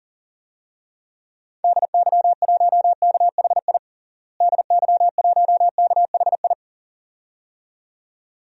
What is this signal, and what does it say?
Morse code, heard as a keyed tone, about 37 wpm, 700 Hz. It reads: DY1KHI DY1KHI
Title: DXCC-100 Call Signs - 6 Characters - Repeated 2x - 40wpm